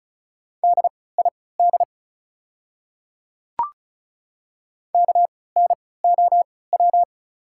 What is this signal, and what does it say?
Morse code, heard as a keyed tone, about 36 wpm, spaced as for 20 wpm, 690 Hz. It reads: DID KNOW
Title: 3x Character Spacing - Top 100 Words - 35wpm